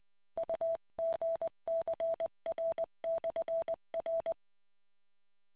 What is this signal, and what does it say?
Morse code, heard as a keyed tone, about 20 wpm, 660 Hz. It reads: UGCR/R